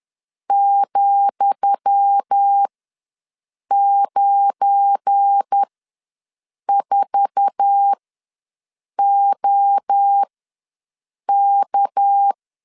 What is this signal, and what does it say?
Morse code, heard as a keyed tone, about 10 wpm, spaced as for 6 wpm, 800 Hz. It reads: ,94OK
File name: LEZIONI / 10 SIGNS / 10D COMA ws 10wpm 4314